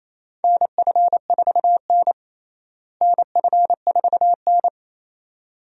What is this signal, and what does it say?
Morse code, heard as a keyed tone, about 28 wpm, 700 Hz. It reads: NF4N NF4N
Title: USA Call Signs - 4 Characters - Repeated 2x - 28wpm